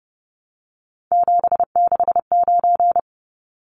Morse code, heard as a keyed tone, about 30 wpm, 700 Hz. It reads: 769
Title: Numbers - 3-Digit 30wpm